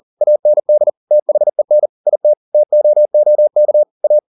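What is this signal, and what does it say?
Morse code, heard as a keyed tone, about 40 wpm, 600 Hz. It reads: AND THEN IT TOOK A
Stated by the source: Morse code at 40 wpm, practice text, CW med Gnister.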